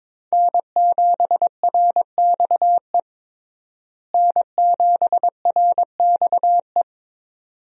Morse code, heard as a keyed tone, about 21 wpm, 700 Hz. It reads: N7RXE N7RXE
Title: USA Call Signs - 5 Characters - Repeated 2x - 22wpm